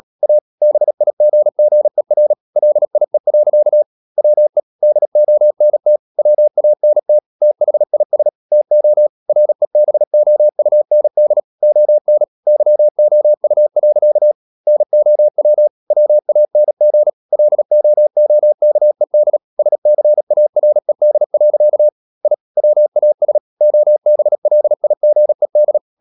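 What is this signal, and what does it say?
Morse code, heard as a keyed tone, about 37 wpm, 600 Hz. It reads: A BIGGER LIE. WE DONT WANT THIS TO REBOUND ON YOU. NOW WANG LOOKED SCARED. I WAS OBLIGED
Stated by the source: Morse code practice at 35–39 wpm sending English text of Ken Follett's novel Never, from CW med Gnister.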